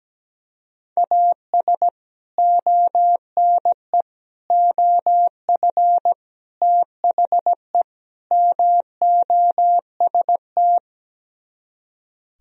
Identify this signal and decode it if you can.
Morse code, heard as a keyed tone, about 17 wpm, 700 Hz. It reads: AS ONE OF THE MOST
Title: Speed-Racing - Sets of 5 Words 17wpm